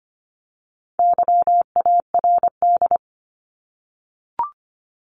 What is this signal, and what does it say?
Morse code, heard as a keyed tone, about 25 wpm, 700 Hz. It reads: YARD
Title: Top 1751-2000 Words 25wpm